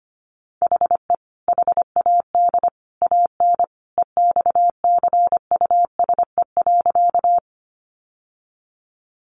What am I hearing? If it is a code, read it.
Morse code, heard as a keyed tone, about 25 wpm, 700 Hz. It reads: HE HAD AN EXCUSE.
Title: Sentences from Top 800 Words 25wpm